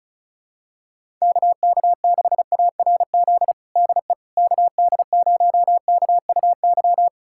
Morse code, heard as a keyed tone, about 36 wpm, 700 Hz. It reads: KK6ARZ DE KD0KUY
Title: USA Callsign DE Callsign 35wpm